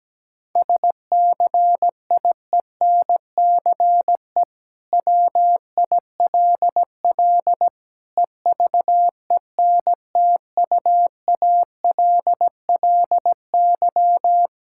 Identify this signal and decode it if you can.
Morse code, heard as a keyed tone, about 17 wpm, 700 Hz. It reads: SCIENCE WILL EVENTUALLY
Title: Sentences from Top 1000 Words - Review Part 6 - 17wpm